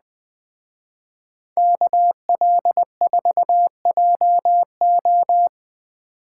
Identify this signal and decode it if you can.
Morse code, heard as a keyed tone, about 19 wpm, 700 Hz. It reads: KL4JO
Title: Speed Racing - USA 2x2 Call Signs 17wpm